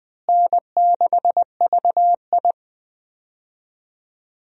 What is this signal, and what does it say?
Morse code, heard as a keyed tone, about 20 wpm, 700 Hz. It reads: N6VI
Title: USA Call Signs - 4 Characters - Repeated 2x - 20wpm